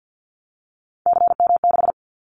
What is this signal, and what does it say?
Morse code, heard as a keyed tone, about 50 wpm, 700 Hz. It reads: CM6
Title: Callsign Prefixes for Top 100 DXCC - 50wpm